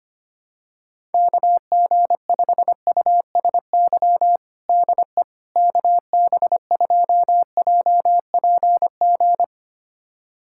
Morse code, heard as a keyed tone, about 25 wpm, 700 Hz. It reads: KG5USY DE KB2JPG
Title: USA Callsign DE Callsign - 6 Characters - 25wpm